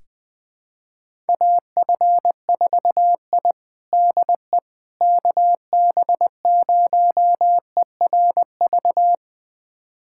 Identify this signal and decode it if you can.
Morse code, heard as a keyed tone, about 20 wpm, 700 Hz. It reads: AF4I DE KB0ERV
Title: USA Callsign DE Callsign 20wpm